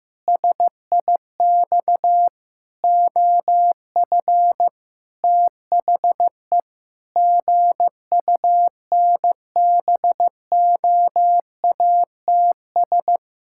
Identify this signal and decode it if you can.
Morse code, heard as a keyed tone, about 15 wpm, 700 Hz. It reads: SIX OF THE GUNBOATS
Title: Pangram Challenge v2 - 15wpm